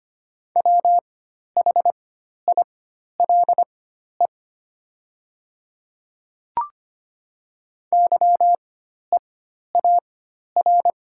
Morse code, heard as a keyed tone, about 25 wpm, spaced as for 12 wpm, 700 Hz. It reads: WHILE YEAR
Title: 4x Character Spacing - Top 500 Words - 25wpm